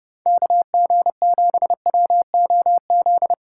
Morse code, heard as a keyed tone, about 30 wpm, 700 Hz. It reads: KG7WOZ
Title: USA Call Signs - 6 Characters - 3x Character Spacing - 30wpm